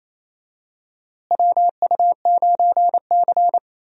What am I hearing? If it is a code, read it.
Morse code, heard as a keyed tone, about 28 wpm, 700 Hz. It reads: WU9C
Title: USA Call Signs - 4 Characters - Repeated 2x - 28wpm